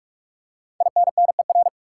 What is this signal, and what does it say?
Morse code, heard as a keyed tone, about 46 wpm, 700 Hz. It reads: INNER
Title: Five Letter Words - 45wpm